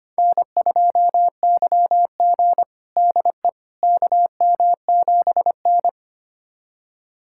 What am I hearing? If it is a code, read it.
Morse code, heard as a keyed tone, about 25 wpm, 700 Hz. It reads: N2YG DE KM7N
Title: USA Callsign DE Callsign - 4 Characters - 25wpm